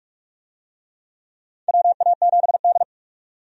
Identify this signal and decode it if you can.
Morse code, heard as a keyed tone, about 45 wpm, 700 Hz. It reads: WA7D
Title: USA Call Signs - 4 Characters - 45wpm